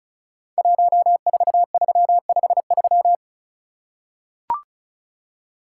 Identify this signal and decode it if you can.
Morse code, heard as a keyed tone, about 36 wpm, 700 Hz. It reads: 14353
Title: Code Groups - 5 Letter:Numbers - 35wpm